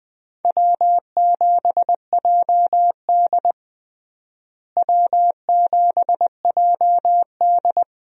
Morse code, heard as a keyed tone, about 20 wpm, 700 Hz. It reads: W7JD W7JD
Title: USA Call Signs - 4 Characters - Repeated 3x - 20wpm